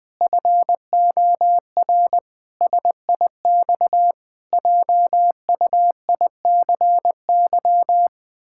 Morse code, heard as a keyed tone, about 21 wpm, 700 Hz. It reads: FOR SIX JUICY
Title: Pangram Challenge v2 - 20wpm